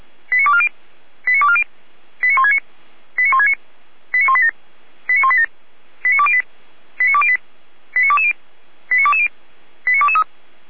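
Five-tone selective call (ZVEI), heard as quick runs of five tones, tones of about 70 ms. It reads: 89240 89240 89169 89169 89187 89187 89208 89208 89220 89220 89223
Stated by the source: ZVEI five-tone call 89230 received by SDR, archived on the Feuerwehr Orlamünde fire brigade server.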